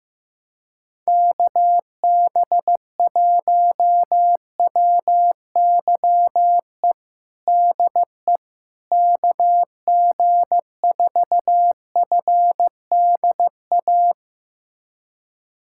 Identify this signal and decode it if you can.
Morse code, heard as a keyed tone, about 15 wpm, 700 Hz. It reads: KB1WYE DE KG4FDA
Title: USA Callsign DE Callsign - 6 Characters - 15wpm